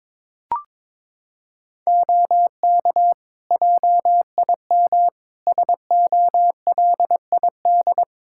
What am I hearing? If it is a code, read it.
Morse code, heard as a keyed tone, about 21 wpm, 700 Hz. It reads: OK JIM SOLID
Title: Practice QSO - Exchange 3 - More Info - 22wpm